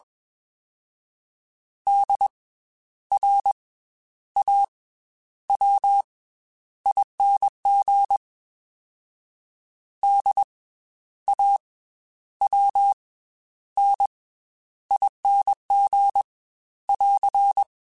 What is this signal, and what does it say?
Morse code, heard as a keyed tone, about 20 wpm, 800 Hz. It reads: D R A W ING D A W N ING <AR>
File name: Special INGs / Special INGs-mp3s / Slow ing @20